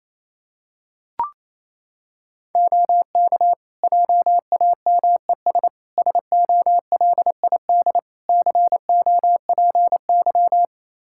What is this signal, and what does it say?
Morse code, heard as a keyed tone, about 28 wpm, 700 Hz. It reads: OK JAMES SOLID COPY
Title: Practice QSO - Exchange 3 - More Info - 28wpm